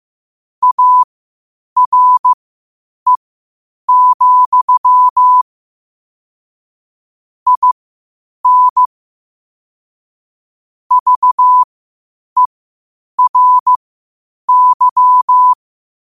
Morse code, heard as a keyed tone, about 15 wpm, spaced as for 8 wpm, 1000 Hz. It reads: ARE, IN VERY
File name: qotdfiles / 8wpm